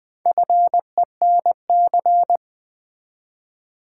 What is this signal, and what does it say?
Morse code, heard as a keyed tone, about 20 wpm, 700 Hz. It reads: FENC
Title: Build A Word - 1 Letter At A Time - 3 to 5-Letter Words - 20wpm